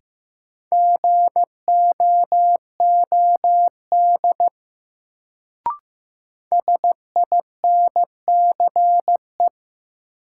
Morse code, heard as a keyed tone, about 15 wpm, 700 Hz. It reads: GOOD SINCE
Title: Sets of 2 Words - New Word Drill - 15wpm